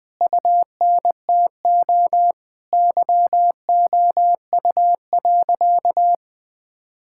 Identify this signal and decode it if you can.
Morse code, heard as a keyed tone, about 20 wpm, 700 Hz. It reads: UNTO YOU.
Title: Sentences with 9-Word Phrases - Spoken Beginning - 20wpm